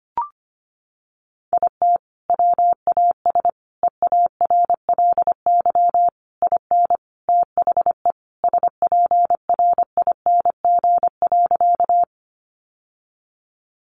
Morse code, heard as a keyed tone, about 25 wpm, 700 Hz. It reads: IT WAS EARLY IN THE SPRING.
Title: Sentences from Top 1200 Words 25wpm